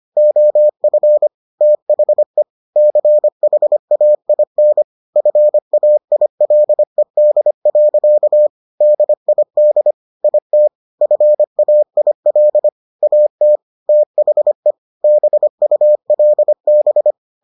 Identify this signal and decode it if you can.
Morse code, heard as a keyed tone, about 25 wpm, 600 Hz. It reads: OF THE CHAIN FAILED. DID IT FAIL AT THE BULB